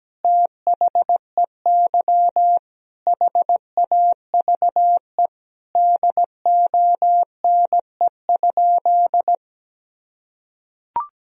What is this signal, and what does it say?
Morse code, heard as a keyed tone, about 17 wpm, 700 Hz. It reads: THEY HAVE DONE?
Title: Sentences from Top 1000 Words - Review Part 3 - 17wpm